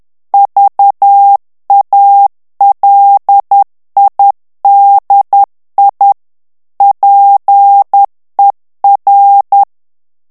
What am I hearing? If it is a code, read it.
Morse code, heard as a keyed tone, about 10 wpm, 800 Hz. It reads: VALIDI PER
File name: morse mp3 / ITALIEN texte clair 1 vitesse 10 mots minute